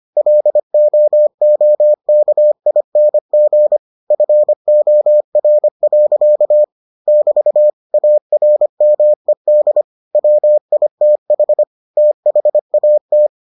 Morse code, heard as a keyed tone, about 25 wpm, 600 Hz. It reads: LOOKING FOR. <BT> ARMED WITH THAT